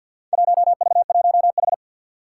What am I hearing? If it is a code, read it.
Morse code, heard as a keyed tone, about 48 wpm, 700 Hz. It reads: 1V1H
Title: Four Challenging Letter-Number Combinations 50wpm